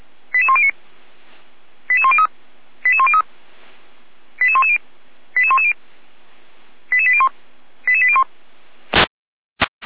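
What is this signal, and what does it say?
Five-tone selective call (ZVEI), heard as quick runs of five tones, tones of about 70 ms. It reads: 88109 88103 88103 88110 88110 88801 88801
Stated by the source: ZVEI five-tone selective call received by SDR, Feuerwehr Orlamünde fire brigade alerting.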